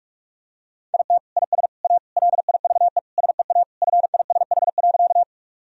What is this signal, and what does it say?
Morse code, heard as a keyed tone, about 48 wpm, 700 Hz. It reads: IT IS A LIVE SEA FISH.
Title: Sentences from Top 1000 Words - Review Part 1 - 45wpm